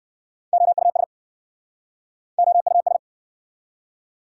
Morse code, heard as a keyed tone, about 59 wpm, 690 Hz. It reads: OUS OUS